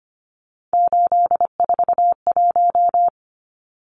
Morse code, heard as a keyed tone, about 25 wpm, 690 Hz. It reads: 841
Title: Numbers - 3-Digit 25wpm